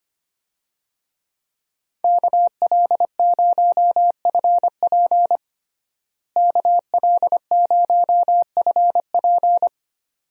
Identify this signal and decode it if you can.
Morse code, heard as a keyed tone, about 25 wpm, 700 Hz. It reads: KL0FP KL0FP
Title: USA Call Signs - 5 Characters - Repeated 2x - 25wpm